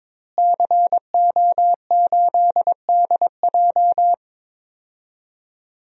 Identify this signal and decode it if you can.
Morse code, heard as a keyed tone, about 21 wpm, 700 Hz. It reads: CO8DJ
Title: DXCC-100 Call Signs - 5 Characters - Repeated 3x - 22wpm